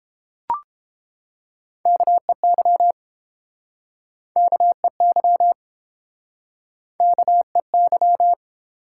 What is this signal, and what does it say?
Morse code, heard as a keyed tone, about 30 wpm, 700 Hz. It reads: KEY KEY KEY